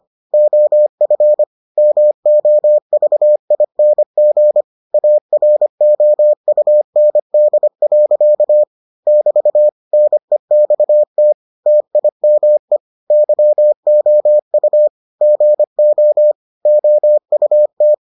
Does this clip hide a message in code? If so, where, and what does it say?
Yes: Morse code from 0.3 to 17.9 s, heard as a keyed tone, about 25 wpm, 600 Hz. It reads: OF MOVING AROUND. <BT> NEXT TIME YOU GO OUT